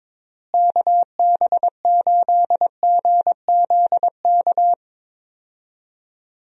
Morse code, heard as a keyed tone, about 21 wpm, 700 Hz. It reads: KB8GZK
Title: USA Call Signs - 6 Characters - Repeated 2x - 22wpm